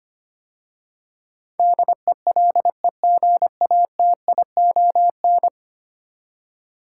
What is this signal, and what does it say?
Morse code, heard as a keyed tone, about 25 wpm, 700 Hz. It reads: DELEGATION
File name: 10 Letter Words - First 3 Letters Given - 25wpm